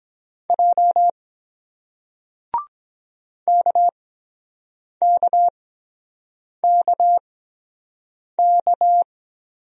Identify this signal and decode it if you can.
Morse code, heard as a keyed tone, about 23 wpm, spaced as for 7 wpm, 700 Hz. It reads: J KKKK